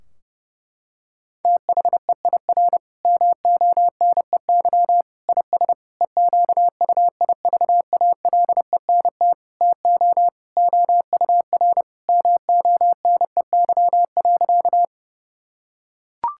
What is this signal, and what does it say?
Morse code, heard as a keyed tone, about 30 wpm, 700 Hz. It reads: THEIR MONEY IS EQUIVALENT TO OUR MONEY.
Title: Sentences from Top 1600 Words 30wpm